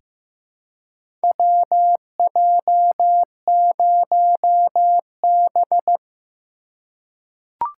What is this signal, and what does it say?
Morse code, heard as a keyed tone, about 15 wpm, 700 Hz. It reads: WJ0B
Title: USA Call Signs - 4 Characters - 4x Character Spacing - 15wpm